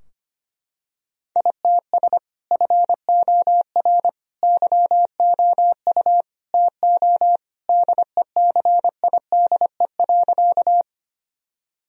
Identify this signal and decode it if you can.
Morse code, heard as a keyed tone, about 25 wpm, 700 Hz. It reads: ITS FOR YOU TO DECIDE.
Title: Sentences from Top 1000 Words - Review Part 3 - 25wpm